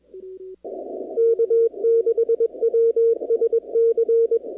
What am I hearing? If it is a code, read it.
Morse code, heard as a keyed tone, about 21 wpm, 460 Hz. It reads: K6WSC
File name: contests / 2019-cw-wpx-kt5j-k5pi- -2330 0-44